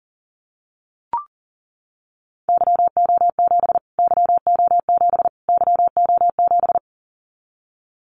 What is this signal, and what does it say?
Morse code, heard as a keyed tone, about 40 wpm, 700 Hz. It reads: YO7 YO7 YO7
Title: Callsign Prefixes for Top 25 DXCC - Repeated 3x - 40wpm